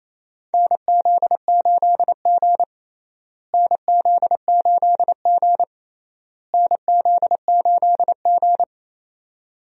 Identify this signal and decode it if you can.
Morse code, heard as a keyed tone, about 28 wpm, 700 Hz. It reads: NZ8G NZ8G NZ8G
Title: USA Call Signs - 4 Characters - Repeated 3x - 28wpm